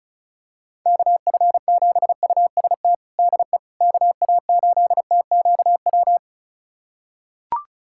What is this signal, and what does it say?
Morse code, heard as a keyed tone, about 36 wpm, 700 Hz. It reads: KF7UST DE KA8TQW
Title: USA Callsign DE Callsign 35wpm